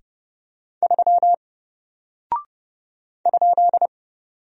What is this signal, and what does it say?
Morse code, heard as a keyed tone, about 30 wpm, 700 Hz. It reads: 3 ?
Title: Single Character (Letter:Number:Punctuation:Prosign) Alt 30wpm